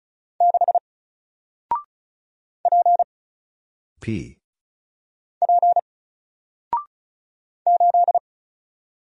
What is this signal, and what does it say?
Morse code, heard as a keyed tone, about 36 wpm, 700 Hz. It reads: 6 P P 8
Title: Single Letter-Number 35wpm